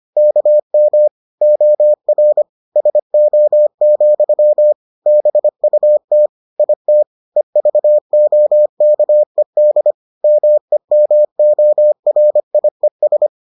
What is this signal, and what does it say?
Morse code, heard as a keyed tone, about 25 wpm, 600 Hz. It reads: KM OR SO, BUT IT EVOKED MEMORIES